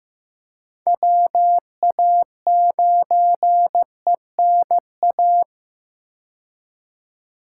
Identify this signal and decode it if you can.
Morse code, heard as a keyed tone, about 15 wpm, 700 Hz. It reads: WA9ENA
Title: K1USN SST QSO Practice - 15wpm